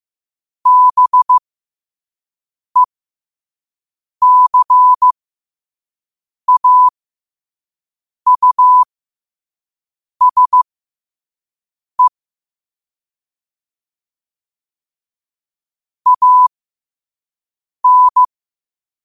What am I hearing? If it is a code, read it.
Morse code, heard as a keyed tone, about 15 wpm, spaced as for 5 wpm, 1000 Hz. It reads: BECAUSE AN
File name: qotdfiles / 5wpm